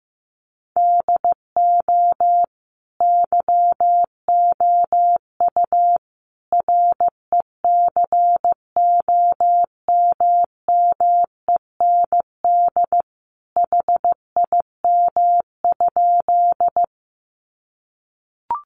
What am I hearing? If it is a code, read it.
Morse code, heard as a keyed tone, about 15 wpm, 700 Hz. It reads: DO YOU RECOMMEND HIM?
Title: Sentences from Top 1200 Words 15wpm